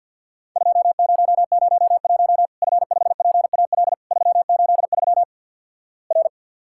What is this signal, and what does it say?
Morse code, heard as a keyed tone, about 50 wpm, 700 Hz. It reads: 2001 F5PAL 383 R
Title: Worked All Europe DX Contest - QTC Exchange - 50wpm